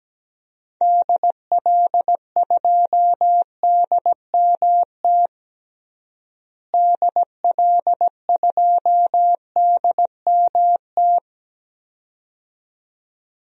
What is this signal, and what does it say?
Morse code, heard as a keyed tone, about 17 wpm, 700 Hz. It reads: DL2DMT DL2DMT
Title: DXCC-100 Call Signs - 6 Characters - Repeated 2x - 17wpm